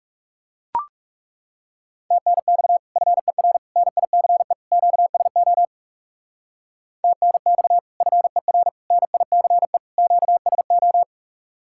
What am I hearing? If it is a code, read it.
Morse code, heard as a keyed tone, about 39 wpm, 700 Hz. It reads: TNX FER NICE QSO TNX FER NICE QSO